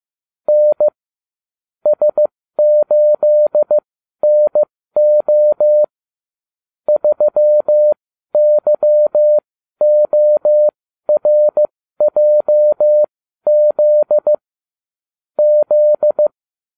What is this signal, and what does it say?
Morse code, heard as a keyed tone, about 15 wpm, spaced as for 12 wpm, 600 Hz. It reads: N S8NO 3YORJZ Z